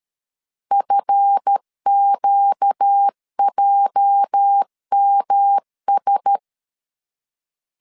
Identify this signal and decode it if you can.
Morse code, heard as a keyed tone, about 13 wpm, 800 Hz. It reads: FQJMS
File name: LEZIONI / 10 SIGNS / 10A SLASH 12wpm 2314